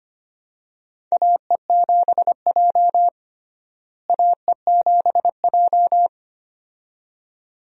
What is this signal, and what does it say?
Morse code, heard as a keyed tone, about 25 wpm, 700 Hz. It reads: AE7J AE7J
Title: USA Call Signs - 4 Characters - Repeated 2x - 25wpm